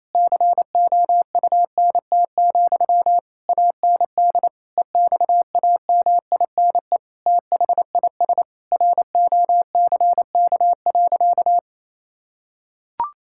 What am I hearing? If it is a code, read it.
Morse code, heard as a keyed tone, about 28 wpm, 700 Hz. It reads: COUNT, AND EXAMINE THIS ROCK.
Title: Sentences from Top 1600 Words 28wpm